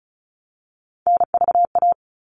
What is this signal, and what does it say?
Morse code, heard as a keyed tone, about 34 wpm, 700 Hz. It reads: NVA
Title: All Cut Numbers - 3-Digit 35wpm